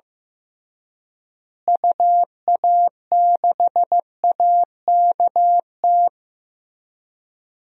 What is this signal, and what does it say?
Morse code, heard as a keyed tone, about 15 wpm, 700 Hz. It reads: UA6AKT
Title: DXCC-100 Call Signs - 6 Characters - Repeated 3x - 15wpm